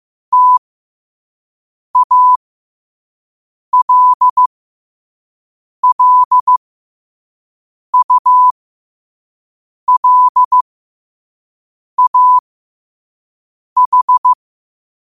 Morse code, heard as a keyed tone, about 15 wpm, spaced as for 5 wpm, 1000 Hz. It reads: TALLULAH
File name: qotdfiles / 5wpm